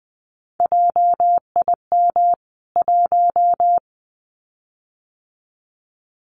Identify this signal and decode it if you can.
Morse code, heard as a keyed tone, about 20 wpm, 700 Hz. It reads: JIM 1
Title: CWT Exchange 20wpm